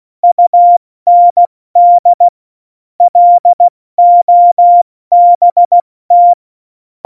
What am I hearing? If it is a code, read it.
Morse code, heard as a keyed tone, about 16 wpm, spaced as for 14 wpm, 700 Hz. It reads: UND LOBT
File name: Morsekurs / Witze / Witz 14 Juli 2020, 70 F80